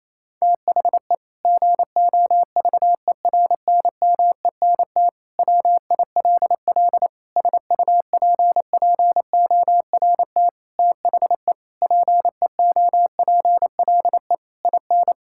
Morse code, heard as a keyed tone, about 27 wpm, 700 Hz. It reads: THE GOVERNMENT WILL SUPPORT THE PEOPLE IN
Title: Sentences from Top 1000 Words - Review Part 5 - 28wpm